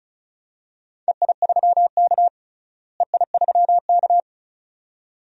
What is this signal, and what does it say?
Morse code, heard as a keyed tone, about 34 wpm, 700 Hz. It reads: EI3K EI3K
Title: DXCC-100 Call Signs - 4 Characters - Repeated 2x - 35wpm